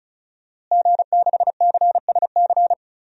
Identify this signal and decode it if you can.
Morse code, heard as a keyed tone, about 34 wpm, 700 Hz. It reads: G6CSC
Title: DXCC-100 Call Signs - 5 Characters - Repeated 3x - 35wpm